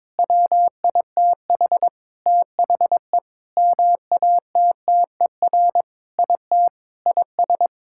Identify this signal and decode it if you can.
Morse code, heard as a keyed tone, about 22 wpm, 700 Hz. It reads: WITH THE MATTER IT IS
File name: Sentences with 7-Word Phrases - 22wpm